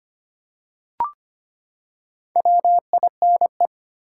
Morse code, heard as a keyed tone, about 25 wpm, 700 Hz. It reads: WINE
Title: Four Letter Words - 25wpm